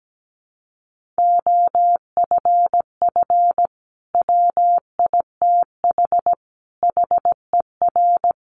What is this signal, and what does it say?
Morse code, heard as a keyed tone, about 17 wpm, 700 Hz. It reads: OFF WITH HER
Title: Sentences from Top 300 Words 17wpm